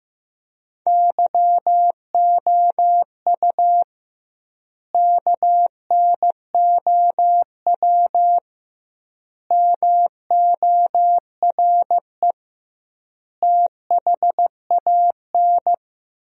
Morse code, heard as a keyed tone, about 15 wpm, 700 Hz. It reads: YOU KNOW MORE THAN
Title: 2x Word-Spacing - Sentences from Top 100 Words - 15wpm